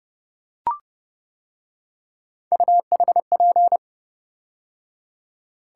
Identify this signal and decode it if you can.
Morse code, heard as a keyed tone, about 28 wpm, 700 Hz. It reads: UHP